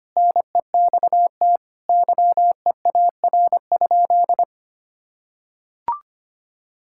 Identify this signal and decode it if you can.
Morse code, heard as a keyed tone, about 25 wpm, 700 Hz. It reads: NEXT YEAR?
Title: Sentences from Top 1000 Words - Review Part 5 - 25wpm